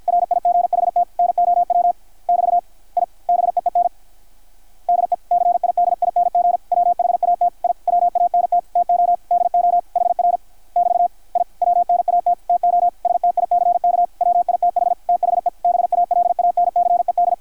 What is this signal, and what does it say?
Morse code, heard as a keyed tone, about 53 wpm, 720 Hz, with radio noise. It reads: RIGHT NOW <BT> I BEEN DECIDING WHAT I WANT TO DO HR <BT> I WNAT TO STICK WITH THE BALANCED
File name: archived / CRSnet